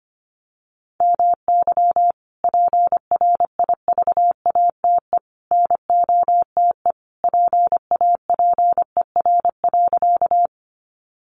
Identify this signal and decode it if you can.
Morse code, heard as a keyed tone, about 25 wpm, 700 Hz. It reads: MY PRIVATE NOTE PAPER.
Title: Sentences from Top 1000 Words 25wpm